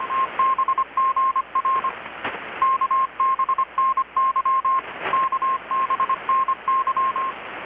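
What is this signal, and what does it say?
Morse code, heard as a keyed tone, about 25 wpm, 1060 Hz. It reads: TBGR KBNY KBNY
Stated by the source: Morse from KPH's Night of Nights XX maritime radiotelegraphy event, received in Chiapas, Mexico about 3,800 km away on a Kenwood R-600 with a grounded 10 m long-wire antenna.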